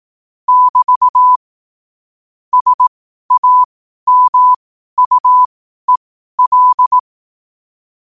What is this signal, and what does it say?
Morse code, heard as a keyed tone, about 18 wpm, spaced as for 12 wpm, 1000 Hz. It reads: <BT> SAMUEL
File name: qotdfiles / 12wpm